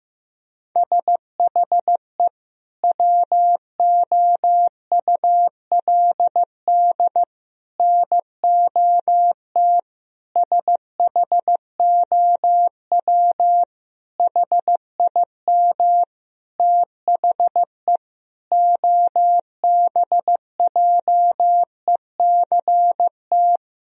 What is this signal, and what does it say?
Morse code, heard as a keyed tone, about 15 wpm, 700 Hz. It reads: SHE WOULD NOT SHOW HIM THE OBJECT
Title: Sentences from Top 1000 Words - Review Part 3 - 15wpm